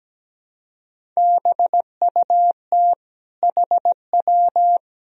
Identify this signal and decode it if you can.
Morse code, heard as a keyed tone, about 17 wpm, 700 Hz. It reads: BUT HW
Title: Sentences from Top 400 Words and Abbreviations 17wpm